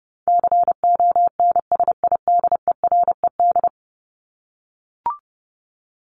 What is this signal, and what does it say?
Morse code, heard as a keyed tone, about 30 wpm, 700 Hz. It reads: CONSIDERED
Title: ED Suffix - Top 100 Words with Base Words 30wpm